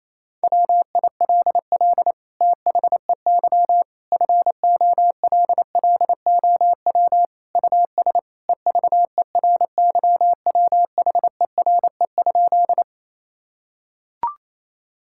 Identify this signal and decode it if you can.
Morse code, heard as a keyed tone, about 28 wpm, 700 Hz. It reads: WILL THEY FOLLOW US EVERYWHERE?
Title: Sentences from Top 1600 Words 28wpm